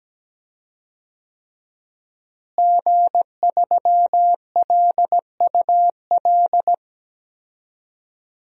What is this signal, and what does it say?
Morse code, heard as a keyed tone, about 17 wpm, 700 Hz. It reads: G3LUL
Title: DXCC-100 Call Signs - 5 Characters - Repeated 3x - 17wpm